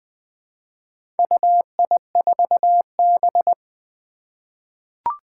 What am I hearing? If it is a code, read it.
Morse code, heard as a keyed tone, about 20 wpm, 700 Hz. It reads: UI4B
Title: Four Challenging Letter-Number Combinations 20wpm